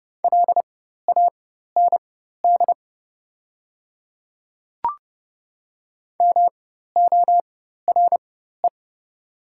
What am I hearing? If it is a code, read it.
Morse code, heard as a keyed tone, about 30 wpm, spaced as for 14 wpm, 690 Hz. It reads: LAND MORE
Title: 4x Character Spacing - Top 200 Words - 30wpm